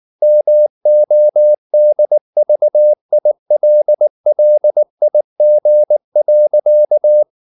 Morse code, heard as a keyed tone, about 19 wpm, 600 Hz. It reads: MODVILLIG.
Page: Ferie 19 wpm